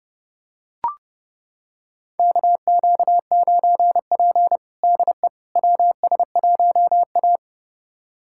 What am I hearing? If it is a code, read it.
Morse code, heard as a keyed tone, about 32 wpm, 700 Hz. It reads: KQ9P DE WS1A
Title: USA Callsign DE Callsign - 4 Characters - 30wpm